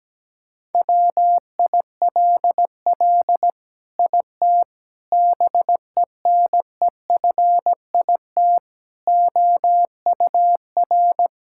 Morse code, heard as a keyed tone, about 17 wpm, 700 Hz. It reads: WILL IT BENEFIT OUR
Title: Sentences from Top 1000 Words - Review Part 6 - 17wpm